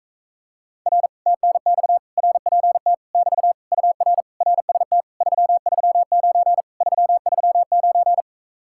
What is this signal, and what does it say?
Morse code, heard as a keyed tone, about 38 wpm, 700 Hz. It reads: R TNX RPT <BT> UR RST 339 339